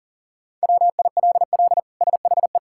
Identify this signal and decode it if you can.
Morse code, heard as a keyed tone, about 40 wpm, 700 Hz. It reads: WILL SHE
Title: Sentences from Top 1000 Words - Review Part 6 - 40wpm